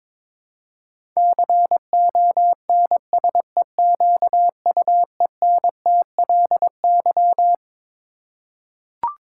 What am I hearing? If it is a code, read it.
Morse code, heard as a keyed tone, about 22 wpm, 700 Hz. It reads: CONSEQUENTLY
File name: Twelve Letter Words - 22wpm